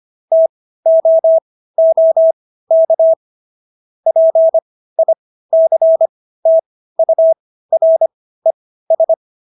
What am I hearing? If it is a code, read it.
Morse code, heard as a keyed tone, about 25 wpm, spaced as for 15 wpm, 650 Hz. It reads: TOOK PICTURES